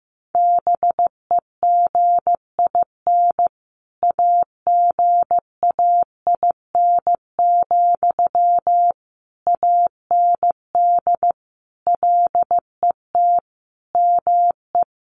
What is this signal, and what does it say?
Morse code, heard as a keyed tone, about 14 wpm, 700 Hz. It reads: BEGIN AGAIN, AND LET ME
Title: Sentences from Top 800 Words 15wpm